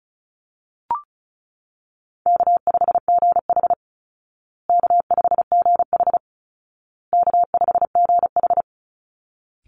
Morse code, heard as a keyed tone, about 34 wpm, 700 Hz. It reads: K5GH K5GH K5GH
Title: Call Signs Repeated 2x - 35wpm